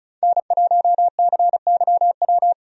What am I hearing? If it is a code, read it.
Morse code, heard as a keyed tone, about 36 wpm, 700 Hz. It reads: N1CYW
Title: USA Call Signs - 5 Characters - 35wpm